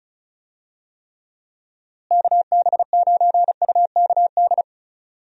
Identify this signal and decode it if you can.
Morse code, heard as a keyed tone, about 36 wpm, 700 Hz. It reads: KB9UKD
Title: USA Call Signs - 6 Characters - Repeated 3x - 35wpm